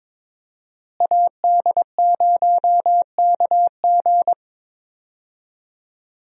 Morse code, heard as a keyed tone, about 22 wpm, 700 Hz. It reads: AD0KG
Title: USA Call Signs - 5 Characters - Repeated 3x - 22wpm